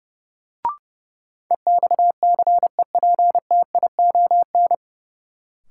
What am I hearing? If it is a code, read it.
Morse code, heard as a keyed tone, about 32 wpm, 700 Hz. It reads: EXCEPTION
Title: Sets of 4 Words - New Word Drill - 30wpm